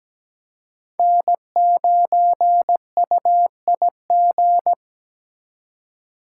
Morse code, heard as a keyed tone, about 20 wpm, 700 Hz. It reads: N9UIG
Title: Speed Racing - USA 1x3 Call Signs 17wpm